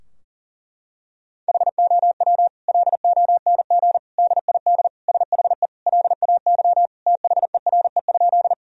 Morse code, heard as a keyed tone, about 40 wpm, 700 Hz. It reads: HOW LONG DID SHE LAY THERE?
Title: Sentences from Top 1000 Words - Review Part 2 - 40wpm